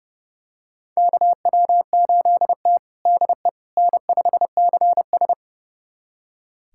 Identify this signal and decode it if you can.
Morse code, heard as a keyed tone, about 30 wpm, 700 Hz. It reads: KW8T DE N5CS
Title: USA Callsign DE Callsign - 4 Characters - 30wpm